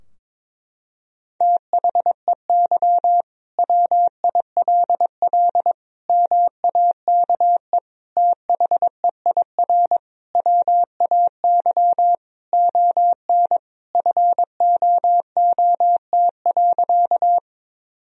Morse code, heard as a keyed tone, about 22 wpm, 700 Hz. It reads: THEY WILL MAKE THEIR WAY ON FOOT.
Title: Sentences from Top 1000 Words - Review Part 3 - 22wpm